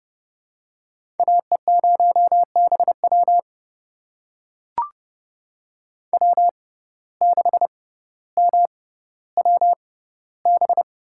Morse code, heard as a keyed tone, about 30 wpm, spaced as for 15 wpm, 700 Hz. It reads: AE0BW W6MWB
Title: USA Call Signs - 5 Characters - 6x Character Spacing - 30wpm